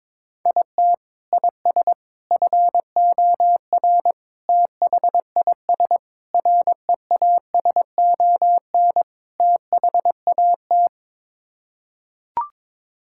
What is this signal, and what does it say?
Morse code, heard as a keyed tone, about 22 wpm, 700 Hz. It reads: IT IS FOR THIS REASON THAT
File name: practice / 22wpm / Sets of 6 Words - 22wpm